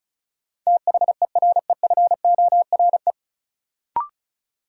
Morse code, heard as a keyed tone, about 34 wpm, 700 Hz. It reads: THEREFORE
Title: Sentences from Top 800 Words - New Word Drill - 35wpm